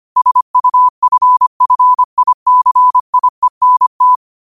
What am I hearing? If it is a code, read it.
Morse code, heard as a keyed tone, about 25 wpm, 1000 Hz. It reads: SUFFICIENT